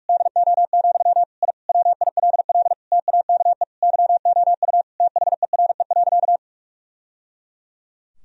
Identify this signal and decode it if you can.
Morse code, heard as a keyed tone, about 46 wpm, 700 Hz. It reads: DO, I WILL TAKE YOU THERE.
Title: Sentences from Top 1000 Words - Review Part 3 - 45wpm